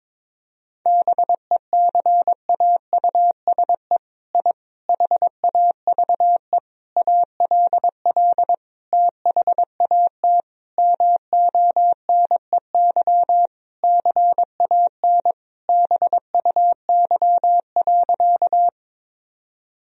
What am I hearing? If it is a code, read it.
Morse code, heard as a keyed tone, about 22 wpm, 700 Hz. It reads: BECAUSE I HAVE ALL THAT MONEY CAN BUY.
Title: Sentences from Top 1000 Words - Review Part 4 - 22wpm